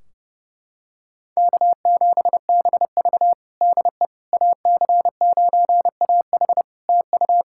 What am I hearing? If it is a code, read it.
Morse code, heard as a keyed tone, about 30 wpm, 700 Hz. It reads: K7BV DE AC9AH TU
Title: Practice QSO - Exchange 4 - Ending - 30wpm